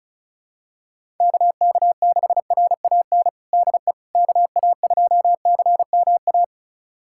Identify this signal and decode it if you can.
Morse code, heard as a keyed tone, about 36 wpm, 700 Hz. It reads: KK6RAN DE KA2CMA
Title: USA Callsign DE Callsign - 6 Characters - 35wpm